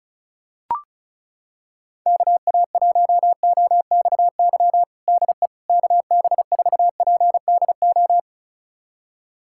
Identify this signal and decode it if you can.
Morse code, heard as a keyed tone, about 36 wpm, 700 Hz. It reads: KA1OXY DE KB4PDO
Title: USA Callsign DE Callsign - 6 Characters - 35wpm